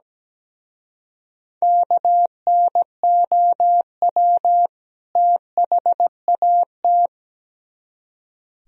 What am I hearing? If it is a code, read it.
Morse code, heard as a keyed tone, about 19 wpm, 700 Hz. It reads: KNOW THAT
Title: Speed-Racing - Sets of 2 Words 17wpm